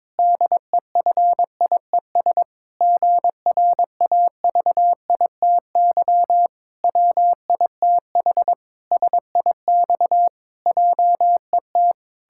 Morse code, heard as a keyed tone, about 22 wpm, 700 Hz. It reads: DEFIES GRAVITY WITH SIX JET
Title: Pangram Challenge v2 - 22wpm